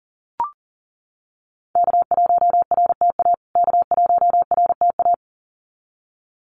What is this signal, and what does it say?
Morse code, heard as a keyed tone, about 40 wpm, 700 Hz. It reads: K1RTA K1RTA
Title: USA Call Signs Repeated 2x 40wpm